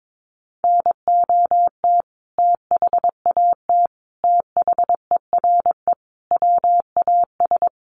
Morse code, heard as a keyed tone, about 22 wpm, 700 Hz. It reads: NOT THAT THERE WAS
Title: Sentences from Top 1400 Words 22wpm